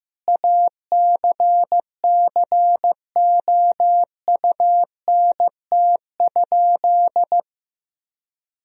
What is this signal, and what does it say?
Morse code, heard as a keyed tone, about 15 wpm, 700 Hz. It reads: ACCOUNT?
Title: Sentences from Top 1000 Words - Review Part 4 - 15wpm